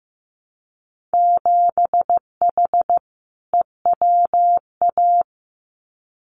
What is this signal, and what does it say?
Morse code, heard as a keyed tone, about 15 wpm, 700 Hz. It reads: 7H EWA
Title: Winter Field Day Exchanges Encore 15wpm